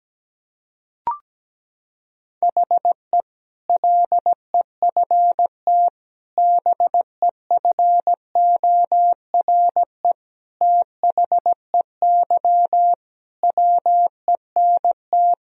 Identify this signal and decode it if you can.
Morse code, heard as a keyed tone, about 17 wpm, 700 Hz. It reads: HE LEFT BEFORE THEY WENT
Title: Sentences from Top 1000 Words - Review Part 4 - 17wpm